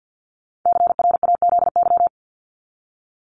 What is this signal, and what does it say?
Morse code, heard as a keyed tone, about 48 wpm, 700 Hz. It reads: CRAZY
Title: Top 1501-1750 Words 50wpm